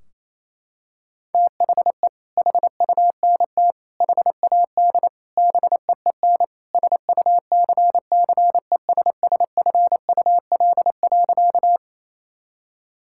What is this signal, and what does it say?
Morse code, heard as a keyed tone, about 28 wpm, 700 Hz. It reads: THE HUNT HAD BEEN SUCCESSFUL.
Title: Sentences from Top 1600 Words 28wpm